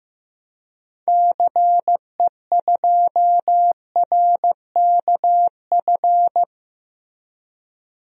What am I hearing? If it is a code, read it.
Morse code, heard as a keyed tone, about 15 wpm, 700 Hz. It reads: CE2RKF
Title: DXCC-100 Call Signs - 6 Characters - Repeated 2x - 15wpm